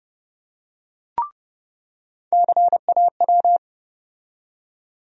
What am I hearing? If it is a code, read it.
Morse code, heard as a keyed tone, about 30 wpm, 700 Hz. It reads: CAW
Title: USA Call Sign Postfixes - 3 Letters 30wpm